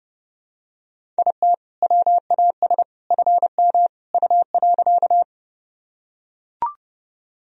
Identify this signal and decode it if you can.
Morse code, heard as a keyed tone, about 30 wpm, 700 Hz. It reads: IT WAS FM U.
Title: Sentences from Top 100 Words and Abbreviations 30wpm